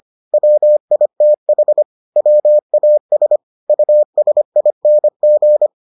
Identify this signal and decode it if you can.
Morse code, heard as a keyed tone, about 25 wpm, 600 Hz. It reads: WITH WAS USING